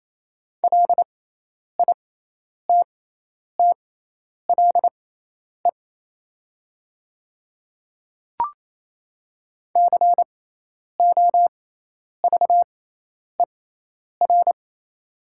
Morse code, heard as a keyed tone, about 28 wpm, spaced as for 10 wpm, 700 Hz. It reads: LITTLE COVER
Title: 6x Character Spacing - Top 300 Words - 28wpm